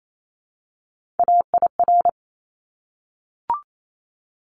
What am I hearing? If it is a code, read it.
Morse code, heard as a keyed tone, about 28 wpm, 700 Hz. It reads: AIR
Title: Top 200 Words 28wpm